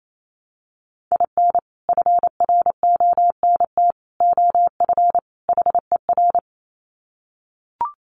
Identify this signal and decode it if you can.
Morse code, heard as a keyed tone, about 28 wpm, 700 Hz. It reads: IN FRONT OF HER
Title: Sets of 4 Words 28wpm